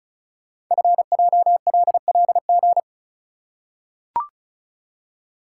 Morse code, heard as a keyed tone, about 36 wpm, 700 Hz. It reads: FJLLG
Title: Code Groups - 5 Letters - 35wpm